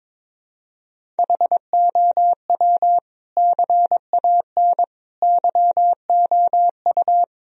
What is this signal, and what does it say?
Morse code, heard as a keyed tone, about 21 wpm, 700 Hz. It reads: HOW CAN YOU
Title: Sentences from Top 1000 Words - Review Part 1 - 22wpm